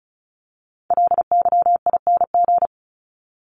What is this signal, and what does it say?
Morse code, heard as a keyed tone, about 34 wpm, 700 Hz. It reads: LYING
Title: ING Suffix - Top 100 Words with Base Words 35wpm